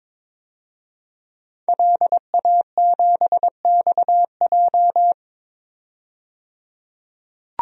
Morse code, heard as a keyed tone, about 22 wpm, 700 Hz. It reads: LA7XJ
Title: DXCC-100 Call Signs - 5 Characters - 22wpm